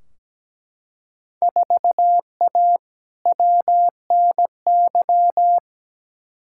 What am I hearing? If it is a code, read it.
Morse code, heard as a keyed tone, about 17 wpm, 700 Hz. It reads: 4A WNY
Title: Field Day Exchanges Encore - 17wpm